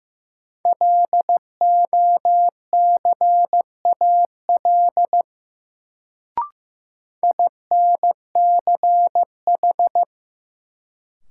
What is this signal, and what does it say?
Morse code, heard as a keyed tone, about 16 wpm, 700 Hz. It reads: LOCAL INCH
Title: Sentences from Top 1000 Words - Review Part 3 - New Word Drill - 15wpm